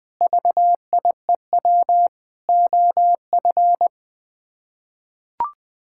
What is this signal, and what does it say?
Morse code, heard as a keyed tone, about 21 wpm, 700 Hz. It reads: VIEW OF
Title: Sets of 6 Words - 20wpm